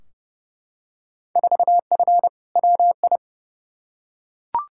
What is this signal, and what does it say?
Morse code, heard as a keyed tone, about 31 wpm, 700 Hz. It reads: 4F WI
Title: Field Day Exchanges - 30wpm